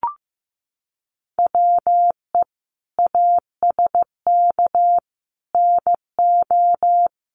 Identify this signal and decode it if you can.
Morse code, heard as a keyed tone, about 15 wpm, 700 Hz. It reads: WE ASK NO
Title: Sentences from Top 1500 Words 15wpm